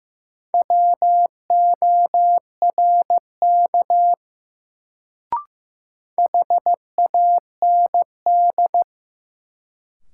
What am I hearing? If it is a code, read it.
Morse code, heard as a keyed tone, about 15 wpm, 700 Hz. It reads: WORK HAND
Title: Sentences from Top 200 Words - New Word Drill - 15wpm